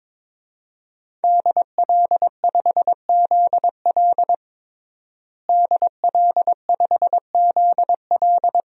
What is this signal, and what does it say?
Morse code, heard as a keyed tone, about 22 wpm, 700 Hz. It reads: DL5ZL DL5ZL
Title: DXCC-100 Call Signs - 5 Characters - Repeated 3x - 22wpm